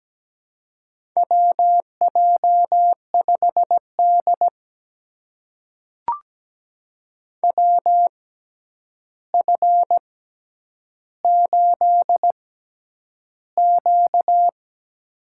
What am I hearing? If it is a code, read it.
Morse code, heard as a keyed tone, about 17 wpm, spaced as for 9 wpm, 700 Hz. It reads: WJ5D WF8Q
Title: USA Call Signs - 4 Characters - 6x Character Spacing - 17wpm